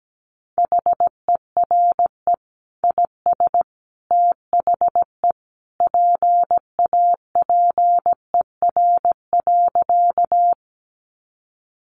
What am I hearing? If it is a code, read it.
Morse code, heard as a keyed tone, about 17 wpm, 700 Hz. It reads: HERE IS THE PAPER.
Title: Sentences from Top 300 Words 17wpm